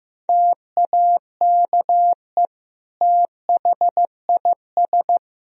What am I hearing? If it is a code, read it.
Morse code, heard as a keyed tone, about 16 wpm, 700 Hz. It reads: TAKE THIS
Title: Sentences from Top 1000 Words - Review Part 4 - 15wpm